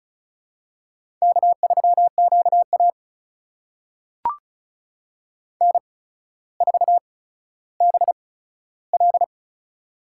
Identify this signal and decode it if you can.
Morse code, heard as a keyed tone, about 36 wpm, spaced as for 15 wpm, 700 Hz. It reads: K3QA N4BL
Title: USA Call Signs - 4 Characters - 8x Character Spacing - 35wpm